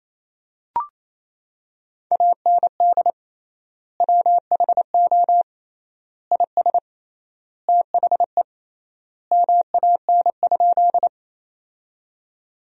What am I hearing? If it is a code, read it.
Morse code, heard as a keyed tone, about 28 wpm, 700 Hz. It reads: AND WHO IS THE MAN?
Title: 3x Word-Spacing - Sentences from Top 200 Words - 28wpm